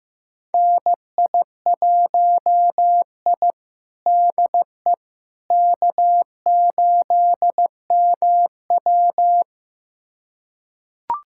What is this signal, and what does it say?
Morse code, heard as a keyed tone, about 16 wpm, 700 Hz. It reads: NI1I DE K8MW
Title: USA Callsign DE Callsign - 4 Characters - 15wpm